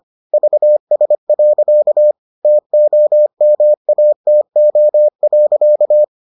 Morse code, heard as a keyed tone, about 25 wpm, 600 Hz. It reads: VS. TOMATO.